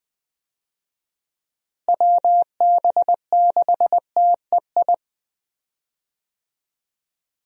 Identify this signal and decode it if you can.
Morse code, heard as a keyed tone, about 20 wpm, 700 Hz. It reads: WB6TEI
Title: USA Call Signs - 6 Characters - Repeated 2x - 20wpm